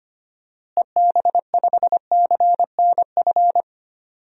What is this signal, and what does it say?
Morse code, heard as a keyed tone, about 25 wpm, 700 Hz. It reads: EB5CNF
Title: DXCC-100 Call Signs - 6 Characters - Repeated 2x - 25wpm